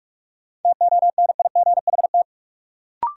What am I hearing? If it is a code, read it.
Morse code, heard as a keyed tone, about 46 wpm, 700 Hz. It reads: TONIGHT
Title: Sentences from Top 600 Words - New Word Drill - 45wpm